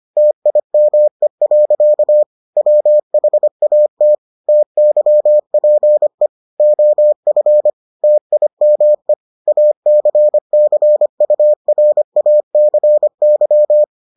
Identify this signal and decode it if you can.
Morse code, heard as a keyed tone, about 25 wpm, 600 Hz. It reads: TIME. WHAT TYPE OF TIME ACCURACY